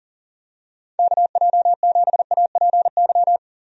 Morse code, heard as a keyed tone, about 40 wpm, 700 Hz. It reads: KJ7APY
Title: USA Call Signs - 6 Characters - 40wpm